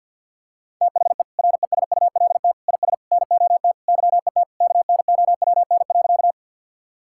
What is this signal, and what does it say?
Morse code, heard as a keyed tone, about 46 wpm, 700 Hz. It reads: THE RESULT IS NOT YET KNOWN.
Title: Sentences from Top 1000 Words - Review Part 5 - 50wpm